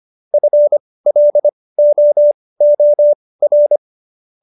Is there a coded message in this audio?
Yes: Morse code from 0.3 to 3.8 s, heard as a keyed tone, about 25 wpm, spaced as for 18 wpm, 590 Hz. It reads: FLOOR